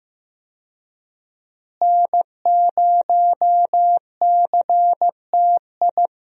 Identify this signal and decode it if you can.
Morse code, heard as a keyed tone, about 15 wpm, 700 Hz. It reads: N0CTI
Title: USA Call Signs - 5 Characters - Repeated 3x - 15wpm